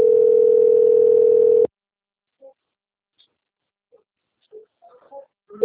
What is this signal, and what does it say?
Ringback tone (North America), 440 + 480 Hz.
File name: out- - - - - .59832